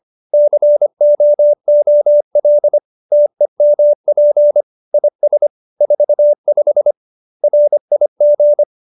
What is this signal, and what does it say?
Morse code, heard as a keyed tone, about 25 wpm, 590 Hz. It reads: COOL TEMP IS 45 RIG
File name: QSO301 25